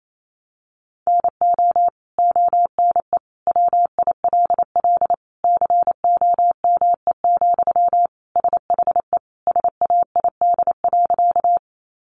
Morse code, heard as a keyed tone, about 28 wpm, 700 Hz. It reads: NO ONE WILL COME, SHE SAID.
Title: Sentences from Top 100 Words 28wpm